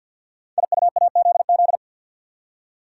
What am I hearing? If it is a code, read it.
Morse code, heard as a keyed tone, about 48 wpm, 700 Hz. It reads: IUAZZ
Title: Code Groups - 5 Letters - 50wpm